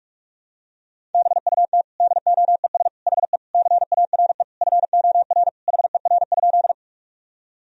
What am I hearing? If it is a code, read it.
Morse code, heard as a keyed tone, about 48 wpm, 700 Hz. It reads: BUT DOES HE CARE FOR HER?
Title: Sentences from Top 1000 Words - Review Part 1 - 45wpm